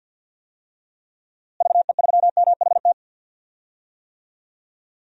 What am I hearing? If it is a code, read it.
Morse code, heard as a keyed tone, about 52 wpm, 700 Hz. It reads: VE3MHT
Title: DXCC-100 Call Signs - 6 Characters - Repeated 3x - 50wpm